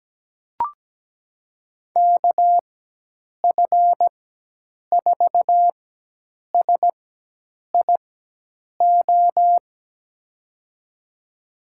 Morse code, heard as a keyed tone, about 17 wpm, spaced as for 8 wpm, 700 Hz. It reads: KF4SIO
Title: USA Call Signs - 4x Character Spacing - 17wpm